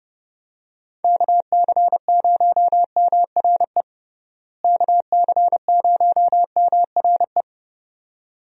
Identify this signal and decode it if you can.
Morse code, heard as a keyed tone, about 32 wpm, 700 Hz. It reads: KC0MRE KC0MRE
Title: USA Call Signs - 6 Characters - Repeated 2x - 30wpm